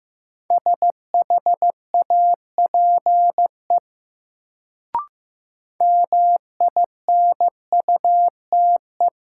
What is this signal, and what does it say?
Morse code, heard as a keyed tone, about 15 wpm, 700 Hz. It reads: SHAPE MINUTE
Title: Sentences from Top 500 Words - New Word Drill - 15wpm